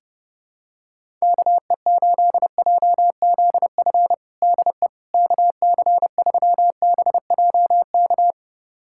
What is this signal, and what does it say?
Morse code, heard as a keyed tone, about 32 wpm, 700 Hz. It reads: KE8JZF DE KC3BJK
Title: USA Callsign DE Callsign - 6 Characters - 30wpm